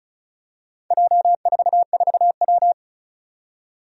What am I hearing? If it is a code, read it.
Morse code, heard as a keyed tone, about 36 wpm, 700 Hz. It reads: J44W
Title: Four Challenging Letter-Number Combinations 35wpm